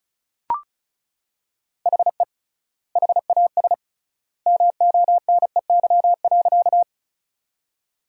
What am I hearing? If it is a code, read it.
Morse code, heard as a keyed tone, about 36 wpm, 700 Hz. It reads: HE HAS MONEY.
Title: 3x Word-Spacing - Sentences from Top 400 Words - 35wpm